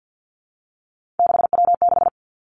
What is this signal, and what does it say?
Morse code, heard as a keyed tone, about 50 wpm, 700 Hz. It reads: 6W6
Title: Three Challenging Letter-Number Combinations 50wpm